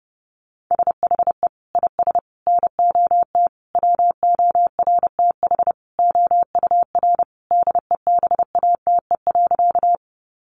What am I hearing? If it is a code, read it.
Morse code, heard as a keyed tone, about 30 wpm, 700 Hz. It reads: SHE IS NOT WORTH OUR DEBATE.
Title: Sentences from Top 1100 Words 30wpm